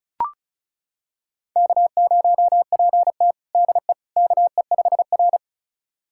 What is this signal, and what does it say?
Morse code, heard as a keyed tone, about 34 wpm, 700 Hz. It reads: K0PT DE KE5R
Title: USA Callsign DE Callsign - 4 Characters - 35wpm